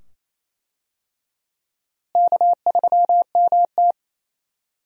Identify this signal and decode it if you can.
Morse code, heard as a keyed tone, about 28 wpm, 700 Hz. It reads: K3MT
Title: USA Call Signs - 4 Characters - Repeated 2x - 28wpm